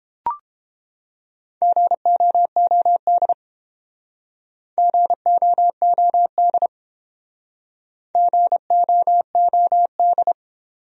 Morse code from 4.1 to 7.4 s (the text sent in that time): GOOD